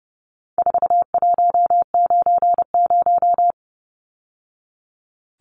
Morse code, heard as a keyed tone, about 30 wpm, 700 Hz. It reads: 4190
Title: Numbers - 4-Digit 30wpm